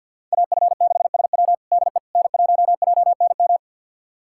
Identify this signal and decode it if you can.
Morse code, heard as a keyed tone, about 49 wpm, 700 Hz. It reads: AF6SW DE N1JNM
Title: USA Callsign DE Callsign - 5 Characters - 50wpm